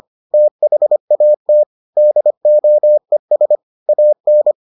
Morse code, heard as a keyed tone, about 25 wpm, 600 Hz. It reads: THAT DOES AN